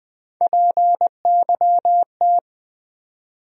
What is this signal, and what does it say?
Morse code, heard as a keyed tone, about 21 wpm, 700 Hz. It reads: PYT
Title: USA Call Sign Postfixes - 3 Letters 20wpm